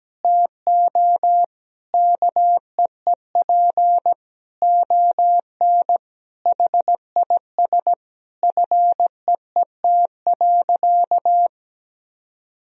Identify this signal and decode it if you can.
Morse code, heard as a keyed tone, about 17 wpm, 700 Hz. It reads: TO KEEP ON HIS FEET.
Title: Sentences from Top 1000 Words - Review Part 1 - 17wpm